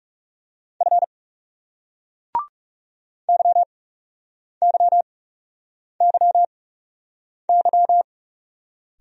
Morse code, heard as a keyed tone, about 41 wpm, spaced as for 8 wpm, 700 Hz. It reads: F YYYY